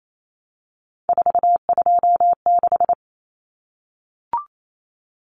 Morse code, heard as a keyed tone, about 28 wpm, 700 Hz. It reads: 426
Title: Common Cut Numbers - 3-Digit 28wpm